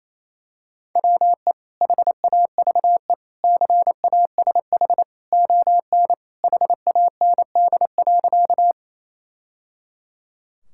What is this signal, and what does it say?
Morse code, heard as a keyed tone, about 27 wpm, 700 Hz. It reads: WE HAVE CASH ON HAND.
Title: Sentences from Top 1000 Words - Review Part 5 - 28wpm